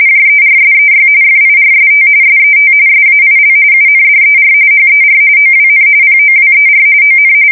RTTY, about 45.45 baud. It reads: JERSEY FIRST SET SAIL UNDER HER OWN POWER ON
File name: fsk170 / USS-New-Jersey